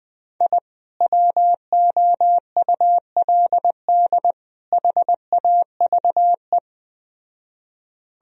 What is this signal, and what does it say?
Morse code, heard as a keyed tone, about 20 wpm, 700 Hz. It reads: I WOULD HAVE